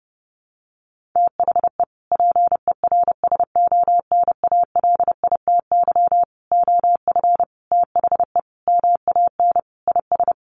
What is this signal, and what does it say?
Morse code, heard as a keyed tone, about 30 wpm, 700 Hz. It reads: THE PERSONALITY OF THE MAN IS
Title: Sentences from Top 1400 Words 30wpm